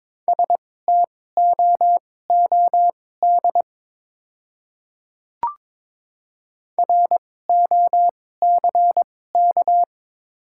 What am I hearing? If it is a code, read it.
Morse code, heard as a keyed tone, about 22 wpm, spaced as for 16 wpm, 700 Hz. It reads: STOOD ROCK
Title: 2x Character Spacing - Top 500 Words - 22wpm